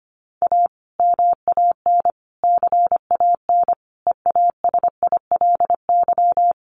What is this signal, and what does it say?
Morse code, heard as a keyed tone, about 25 wpm, 700 Hz. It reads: A MAN CAN EASILY
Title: Sentences from Top 900 Words 25wpm